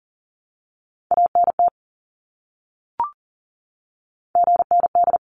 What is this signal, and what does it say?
Morse code, heard as a keyed tone, about 40 wpm, 700 Hz. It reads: ANT GND
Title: Abbreviations Repeated 3x - 40wpm